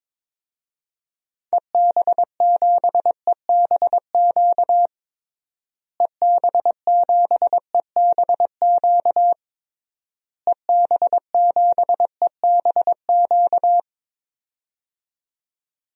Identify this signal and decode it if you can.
Morse code, heard as a keyed tone, about 22 wpm, 700 Hz. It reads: EB7EBQ EB7EBQ EB7EBQ
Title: DXCC-100 Call Signs - 6 Characters - Repeated 3x - 22wpm